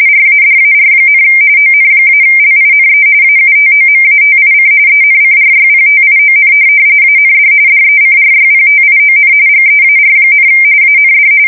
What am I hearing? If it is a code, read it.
RTTY, about 45.45 baud. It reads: RADAR, ON-LINE AUTOMATIC ENCRYPTION AND DECRYPTION GEAR, AND
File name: fsk170 / USS-New-Jersey